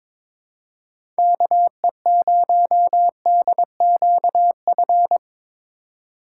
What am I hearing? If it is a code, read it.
Morse code, heard as a keyed tone, about 22 wpm, 700 Hz. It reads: KE0DQF
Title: USA Call Signs - 6 Characters - 3x Character Spacing - 22wpm